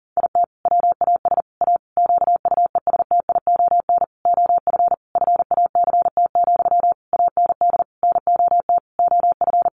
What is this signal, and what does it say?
Morse code, heard as a keyed tone, about 40 wpm, 700 Hz. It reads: IT WAS A QUESTION OF FACT, AND NOT OF
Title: Sentences from Top 1400 Words 40wpm